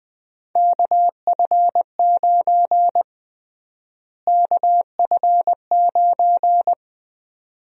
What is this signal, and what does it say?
Morse code, heard as a keyed tone, about 20 wpm, 700 Hz. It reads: KF9 KF9
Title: USA Call Sign Prefixes - Repeated 2x - 20wpm